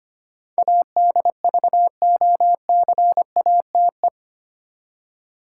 Morse code, heard as a keyed tone, about 25 wpm, 700 Hz. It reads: ADVOCATE
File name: Eight Letter Words - 25wpm